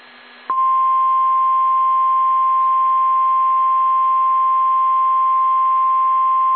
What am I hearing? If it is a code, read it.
Emergency Alert System (SAME) tones (the 1050 Hz attention tone). It is heard from 1.2 to 6.1 s.